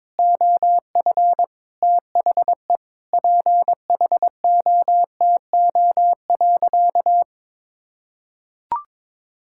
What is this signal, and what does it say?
Morse code, heard as a keyed tone, about 22 wpm, 700 Hz. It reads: OF THE PHOTO.
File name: practice / 22wpm / Sentences with 4-Word Phrases - 22wpm